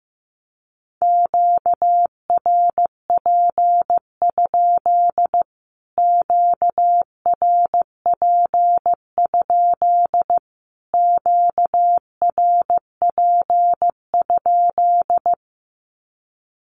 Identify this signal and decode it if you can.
Morse code, heard as a keyed tone, about 15 wpm, 700 Hz. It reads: QRP? QRP? QRP?
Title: Q-Codes Repeated 3x 15wpm